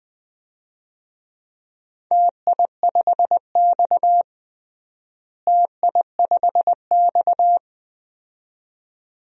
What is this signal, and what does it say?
Morse code, heard as a keyed tone, about 21 wpm, 700 Hz. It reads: TI5X TI5X
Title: DXCC-100 Call Signs - 4 Characters - Repeated 2x - 20wpm